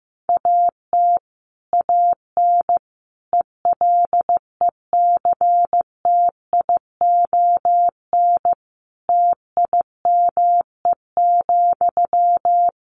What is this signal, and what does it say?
Morse code, heard as a keyed tone, about 15 wpm, 700 Hz. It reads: AT AN ELECTION TIME,
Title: Sentences from Top 1500 Words 15wpm